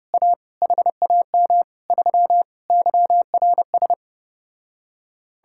Morse code, heard as a keyed tone, about 31 wpm, 700 Hz. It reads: A HAM 3 YRS